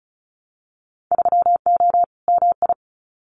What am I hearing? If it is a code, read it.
Morse code, heard as a keyed tone, about 36 wpm, 700 Hz. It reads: 3O MI
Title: Winter Field Day Exchanges Encore 35wpm